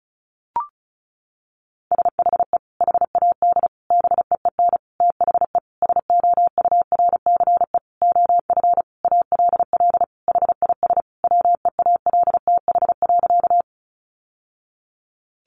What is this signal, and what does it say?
Morse code, heard as a keyed tone, about 34 wpm, 700 Hz. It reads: SHE HAD BEEN THE SOURCE OF ALL HIS WEALTH.
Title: Sentences from Top 1200 Words 35wpm